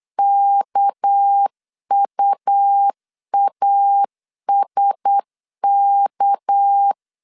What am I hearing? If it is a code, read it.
Morse code, heard as a keyed tone, about 8 wpm, 800 Hz. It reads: KUASK
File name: LEZIONI / 6 PKX0 / 6 PK 8wpm 3003